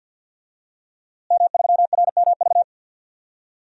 Morse code, heard as a keyed tone, about 49 wpm, 700 Hz. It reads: M3RMV
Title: DXCC-100 Call Signs - 5 Characters - Repeated 2x - 50wpm